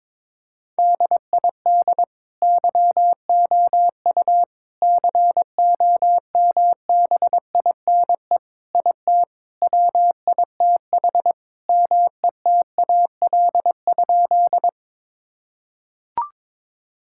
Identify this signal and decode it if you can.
Morse code, heard as a keyed tone, about 22 wpm, 700 Hz. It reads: DID YOU COMBINE IT WITH METAL?
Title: Sentences from Top 1600 Words 22wpm